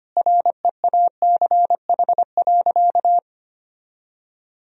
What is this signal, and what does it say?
Morse code, heard as a keyed tone, about 25 wpm, 700 Hz. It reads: REACH.
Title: Sentences from Top 1000 Words - Review Part 4 - 25wpm